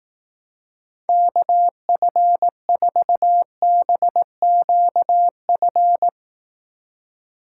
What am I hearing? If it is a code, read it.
Morse code, heard as a keyed tone, about 18 wpm, 700 Hz. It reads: KF4BQF
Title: Speed Racing - USA 2x3 Call Signs 15wpm